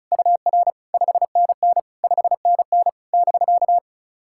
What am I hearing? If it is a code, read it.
Morse code, heard as a keyed tone, about 36 wpm, 690 Hz. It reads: UR 5NN 5NN <BK>
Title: SOTA QSO Practice 35wpm